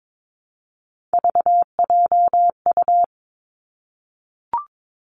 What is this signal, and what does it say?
Morse code, heard as a keyed tone, about 22 wpm, 700 Hz. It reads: VJU
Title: Three Challenging Letter-Number Combinations 22wpm